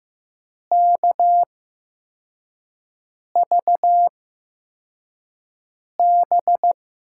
Morse code, heard as a keyed tone, about 15 wpm, spaced as for 4 wpm, 700 Hz. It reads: KVB
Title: USA Call Signs - 6 Characters - 8x Character Spacing - 15wpm